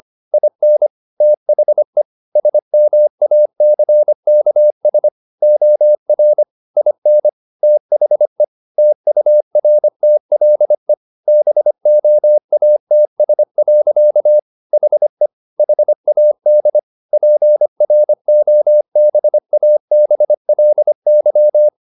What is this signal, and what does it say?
Morse code, heard as a keyed tone, about 25 wpm, 600 Hz. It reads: IN THE SMACKS OR IN THE TURTLE BOATS. HE HAD PROBABLY
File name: the old man and the sea part 1 25wpm